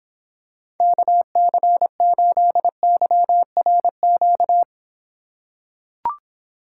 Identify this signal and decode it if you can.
Morse code, heard as a keyed tone, about 26 wpm, 700 Hz. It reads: KC8YRQ